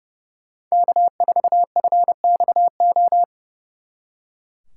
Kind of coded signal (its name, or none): Morse code